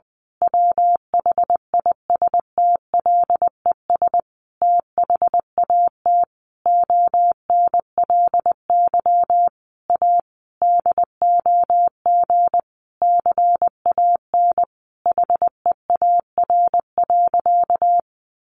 Morse code, heard as a keyed tone, about 20 wpm, 700 Hz. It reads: WHISTLES THAT ONLY A DOG CAN HEAR.
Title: A Sound Of Thunder 20wpm